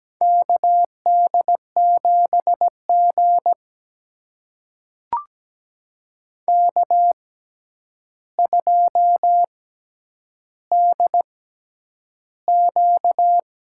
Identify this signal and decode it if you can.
Morse code, heard as a keyed tone, about 17 wpm, spaced as for 9 wpm, 700 Hz. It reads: KD7G K2DQ
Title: USA Call Signs - 6x Character Spacing - 17wpm